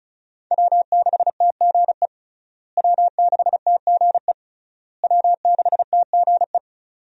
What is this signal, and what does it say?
Morse code, heard as a keyed tone, about 36 wpm, 700 Hz. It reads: W6TGE W6TGE W6TGE
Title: USA Call Signs - 5 Characters - Repeated 3x - 35wpm